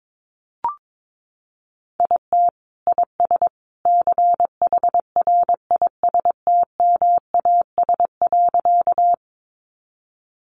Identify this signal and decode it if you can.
Morse code, heard as a keyed tone, about 22 wpm, 700 Hz. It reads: IT IS CHRISTMAS.
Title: Easy Phrases 22wpm